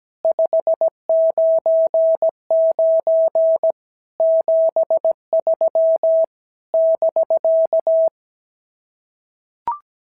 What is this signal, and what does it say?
Morse code, heard as a keyed tone, about 17 wpm, 650 Hz. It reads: 599 73 <BK>
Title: SOTA QSO Practice 17wpm